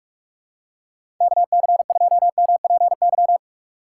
Morse code, heard as a keyed tone, about 46 wpm, 700 Hz. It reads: KC2MPY
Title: USA Call Signs - 2x Character Spacing - 45wpm